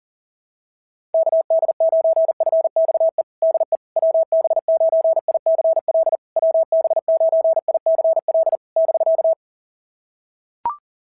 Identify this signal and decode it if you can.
Morse code, heard as a keyed tone, about 40 wpm, 650 Hz. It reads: KD9FXE DE WB9ICL WB9ICL <BK>
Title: SOTA QSO Practice 40wpm